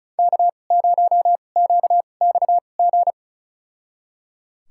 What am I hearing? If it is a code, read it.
Morse code, heard as a keyed tone, about 36 wpm, spaced as for 25 wpm, 700 Hz. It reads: K0QXG
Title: USA Call Signs - 5 Characters - 2x Character Spacing - 35wpm